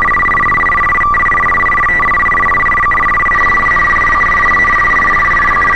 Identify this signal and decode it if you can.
Two-tone teleprinter signal, about 850 Hz shift.